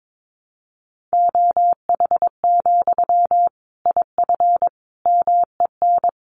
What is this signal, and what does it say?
Morse code, heard as a keyed tone, about 22 wpm, 700 Hz. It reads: OH, IF MEN
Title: Sentences from Top 500 Words 22wpm